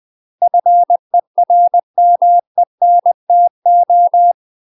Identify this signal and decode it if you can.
Morse code, heard as a keyed tone, about 20 wpm, 700 Hz. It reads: FERMENTO
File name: UK MASTER QSOs TRAINING / 20WPM / UK MASTER QSOs 0062